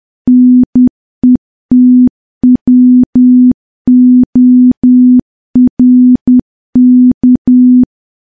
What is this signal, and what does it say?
Morse code, heard as a keyed tone, about 10 wpm, 250 Hz. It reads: NETWORK
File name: morse / morse-training-10WPM